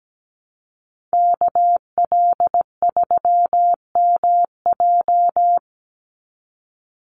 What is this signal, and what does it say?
Morse code, heard as a keyed tone, about 17 wpm, 700 Hz. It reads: KL3MJ
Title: USA Call Signs 17wpm